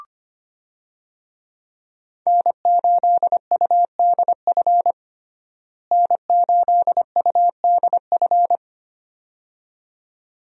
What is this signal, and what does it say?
Morse code, heard as a keyed tone, about 25 wpm, 700 Hz. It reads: N8UDF N8UDF
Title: USA Call Signs - 5 Characters - Repeated 2x - 25wpm